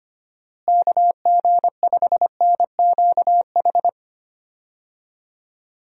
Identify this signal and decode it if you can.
Morse code, heard as a keyed tone, about 25 wpm, 700 Hz. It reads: KG5NQH
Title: USA Call Signs - 6 Characters - Repeated 2x - 25wpm